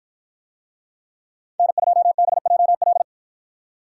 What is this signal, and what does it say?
Morse code, heard as a keyed tone, about 49 wpm, 700 Hz. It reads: N2BJL